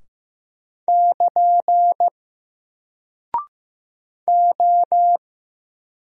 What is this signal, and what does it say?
Morse code, heard as a keyed tone, about 15 wpm, 700 Hz. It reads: <KN> O
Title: Single Character (Letter:Number:Punctuation:Prosign) 15wpm